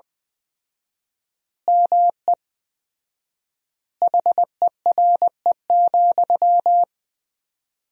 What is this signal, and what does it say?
Morse code, heard as a keyed tone, about 20 wpm, 700 Hz. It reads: ME HERE,
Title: 4x Word-Spacing - Sentences from Top 500 Words - 20wpm